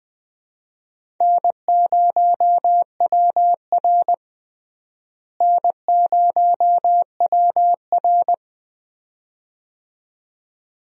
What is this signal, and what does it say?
Morse code, heard as a keyed tone, about 20 wpm, 700 Hz. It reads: N0WR N0WR
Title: USA Call Signs - 4 Characters - Repeated 2x - 20wpm